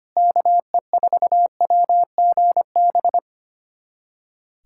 Morse code, heard as a keyed tone, about 25 wpm, 700 Hz. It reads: KE4WGB
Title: USA Callsign DE Callsign 25wpm